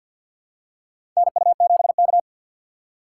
Morse code, heard as a keyed tone, about 52 wpm, 700 Hz. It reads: NU7K
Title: USA Call Signs - 4 Characters - 3x Character Spacing - 50wpm